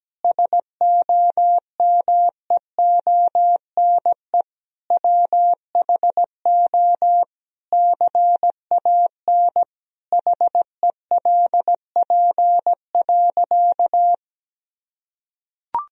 Morse code, heard as a keyed tone, about 17 wpm, 700 Hz. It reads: SOMEONE WHO CAN HELP.
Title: Sentences from Top 1000 Words - Review Part 4 - 17wpm